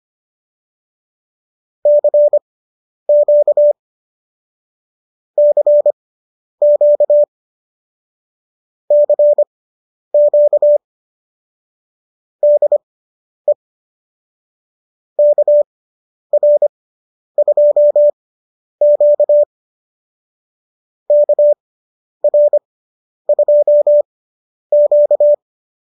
Morse code, heard as a keyed tone, about 25 wpm, spaced as for 10 wpm, 590 Hz. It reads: CQ CQ CQ DE KR2Q KR2Q